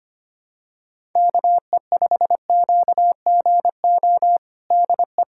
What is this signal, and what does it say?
Morse code, heard as a keyed tone, about 25 wpm, 700 Hz. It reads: KE5QGO DE
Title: USA Callsign DE Callsign - 6 Characters - 25wpm